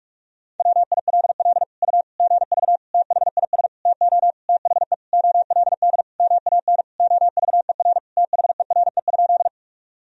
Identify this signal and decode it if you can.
Morse code, heard as a keyed tone, about 46 wpm, 700 Hz. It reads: WILL U GV THIS TO THE OLD MAN OVER THERE?
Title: Sentences from Top 200 Words and Abbreviations 45wpm